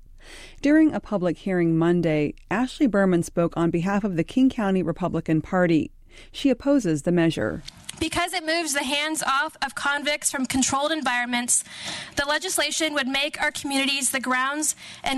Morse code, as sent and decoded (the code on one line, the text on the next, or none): none